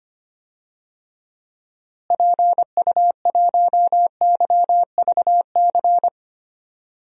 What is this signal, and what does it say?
Morse code, heard as a keyed tone, about 25 wpm, 700 Hz. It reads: PU1YVC
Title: DXCC-100 Call Signs - 6 Characters - Repeated 3x - 25wpm